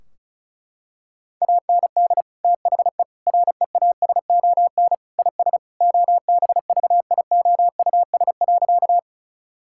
Morse code, heard as a keyed tone, about 36 wpm, 700 Hz. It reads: AND THE REASON IS OBVIOUS.
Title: Sentences from Top 1600 Words 35wpm